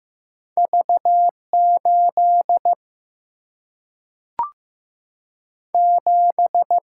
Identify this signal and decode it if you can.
Morse code, heard as a keyed tone, about 15 wpm, 700 Hz. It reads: V 8 7
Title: Single-Letter Single-Number 15wpm